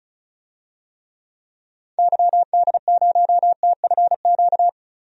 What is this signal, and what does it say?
Morse code, heard as a keyed tone, about 36 wpm, 700 Hz. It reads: YD0TFQ
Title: DXCC-100 Call Signs - 6 Characters - Repeated 3x - 35wpm